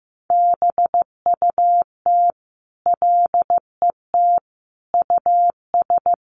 Morse code, heard as a keyed tone, about 14 wpm, 700 Hz. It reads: BUT LET US
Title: Sentences from Top 300 Words 15wpm